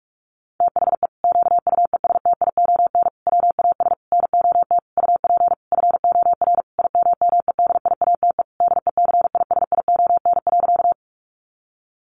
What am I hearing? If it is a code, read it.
Morse code, heard as a keyed tone, about 44 wpm, 700 Hz. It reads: THE QUESTION WAS NOT UP FOR IMMEDIATE DECISION.
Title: Sentences from Top 1200 Words 45wpm